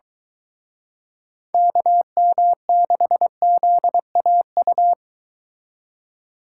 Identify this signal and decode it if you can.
Morse code, heard as a keyed tone, about 23 wpm, 700 Hz. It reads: KM6ZAU